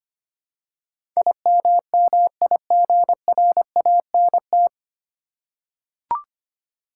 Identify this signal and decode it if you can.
Morse code, heard as a keyed tone, about 25 wpm, 700 Hz. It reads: IMMIGRANT
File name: Nine Letter Words - 25wpm